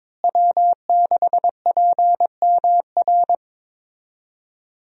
Morse code, heard as a keyed tone, about 22 wpm, 700 Hz. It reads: W6PMR
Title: USA Call Signs - 5 Characters - Repeated 3x - 22wpm